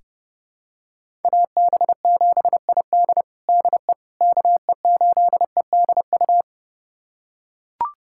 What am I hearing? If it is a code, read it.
Morse code, heard as a keyed tone, about 32 wpm, 700 Hz. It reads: AB7ID DE KE8EDU
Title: USA Callsign DE Callsign 30wpm